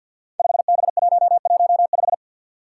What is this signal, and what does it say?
Morse code, heard as a keyed tone, about 49 wpm, 700 Hz. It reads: 5B115
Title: Five Challenging Letter-Number Combinations 50wpm